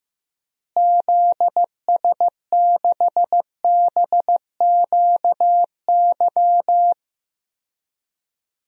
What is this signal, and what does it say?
Morse code, heard as a keyed tone, about 15 wpm, 700 Hz. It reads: ZS6BQY
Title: DXCC-100 Call Signs - 6 Characters - Repeated 3x - 15wpm